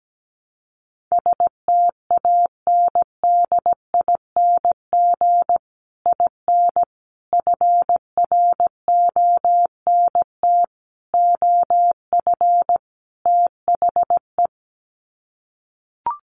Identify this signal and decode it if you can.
Morse code, heard as a keyed tone, about 17 wpm, 700 Hz. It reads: STANDING IN FRONT OF THE
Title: Sets of 5 Words Encore2 17wpm